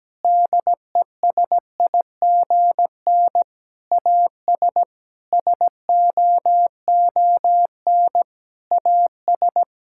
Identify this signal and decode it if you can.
Morse code, heard as a keyed tone, about 17 wpm, 700 Hz. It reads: DESIGN AS SOON AS
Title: Sentences from Top 1000 Words - Review Part 5 - 17wpm